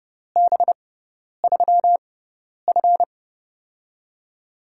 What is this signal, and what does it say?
Morse code, heard as a keyed tone, about 30 wpm, spaced as for 10 wpm, 700 Hz. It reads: B3F
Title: USA Call Signs - 4 Characters - 6x Character Spacing - 30wpm